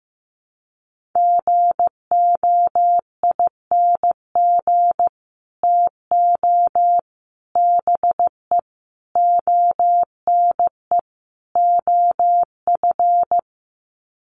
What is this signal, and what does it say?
Morse code, heard as a keyed tone, about 15 wpm, 700 Hz. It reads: GOING TO BE ONE OF
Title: Sets of 5 Words Encore2 15wpm